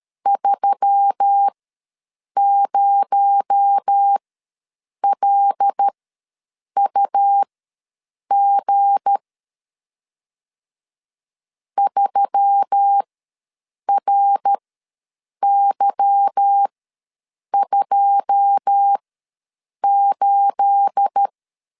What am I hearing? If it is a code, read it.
Morse code, heard as a keyed tone, about 13 wpm, spaced as for 7 wpm, 800 Hz. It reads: 30LUG 3RY28
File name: LEZIONI / 9 28 ws 12wpm 3354